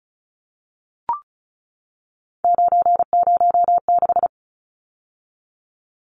Morse code, heard as a keyed tone, about 34 wpm, 700 Hz. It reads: 906
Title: Numbers - 3-Digit 35wpm